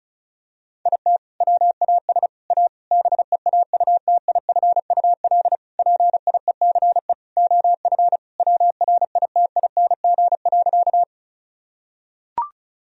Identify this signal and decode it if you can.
Morse code, heard as a keyed tone, about 34 wpm, 700 Hz. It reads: IT WAS A BEAUTIFUL PIECE OF WRITING.
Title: Sentences from Top 1000 Words - Review Part 2 - 35wpm